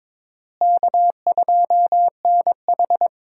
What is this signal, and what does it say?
Morse code, heard as a keyed tone, about 22 wpm, 700 Hz. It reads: K2NH
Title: USA Call Signs - 4 Characters - Repeated 3x - 22wpm